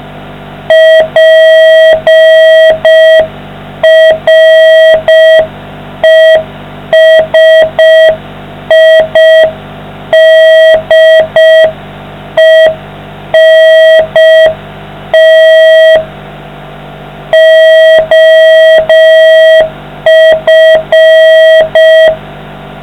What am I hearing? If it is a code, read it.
Morse code, heard as a keyed tone, about 5 wpm, 640 Hz. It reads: PRESIDENT OF